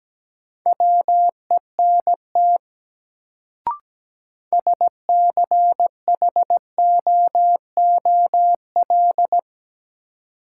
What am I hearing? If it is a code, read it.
Morse code, heard as a keyed tone, about 17 wpm, 700 Hz. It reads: WENT SCHOOL
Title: Sets of 2 Words - New Word Drill - 17wpm